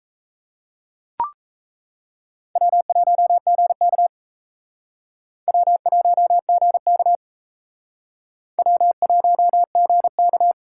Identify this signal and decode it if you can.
Morse code, heard as a keyed tone, about 34 wpm, 700 Hz. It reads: W1GK W1GK W1GK
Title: Speed Racing - USA 1x2 Call Signs 28wpm